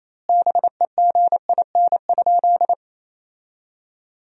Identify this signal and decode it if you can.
Morse code, heard as a keyed tone, about 28 wpm, 700 Hz. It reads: BEGIN?
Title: 2x Word-Spacing - Sentences from Top 300 Words - 28wpm